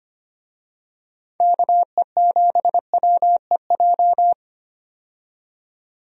Morse code, heard as a keyed tone, about 25 wpm, 700 Hz. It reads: KE7WEJ
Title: USA Call Signs - 6 Characters - 25wpm